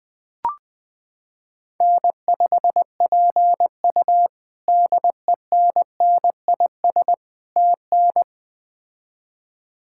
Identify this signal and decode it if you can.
Morse code, heard as a keyed tone, about 20 wpm, 700 Hz. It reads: N5PU DENNIS TN
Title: K1USN SST QSO Practice - 20wpm